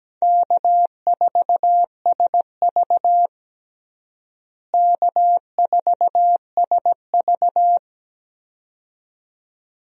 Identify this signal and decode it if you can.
Morse code, heard as a keyed tone, about 17 wpm, 700 Hz. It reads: K4SV K4SV
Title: USA Call Signs - 4 Characters - Repeated 2x - 17wpm